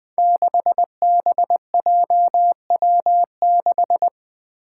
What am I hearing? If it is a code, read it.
Morse code, heard as a keyed tone, about 20 wpm, 700 Hz. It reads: 6BJW6
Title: Five Challenging Letter-Number Combinations 20wpm